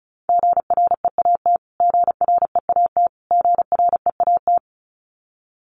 Morse code, heard as a keyed tone, about 34 wpm, 700 Hz. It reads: GREAT GREAT GREAT
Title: Top 200 Words Repeated 35wpm